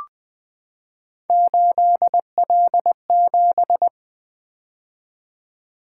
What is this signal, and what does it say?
Morse code, heard as a keyed tone, about 20 wpm, 700 Hz. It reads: 8L7
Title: Three Letter-Number Combinations - 20wpm